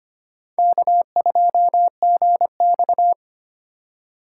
Morse code, heard as a keyed tone, about 32 wpm, 700 Hz. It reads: K2GX